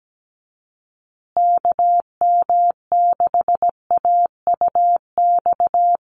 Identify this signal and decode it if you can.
Morse code, heard as a keyed tone, about 17 wpm, 700 Hz. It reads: KM6AUX
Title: USA Call Signs Repeated 2x 17wpm